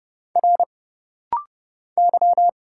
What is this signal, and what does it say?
Morse code, heard as a keyed tone, about 30 wpm, 700 Hz. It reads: R Y
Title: Single Letters - Rapid-Fire 30wpm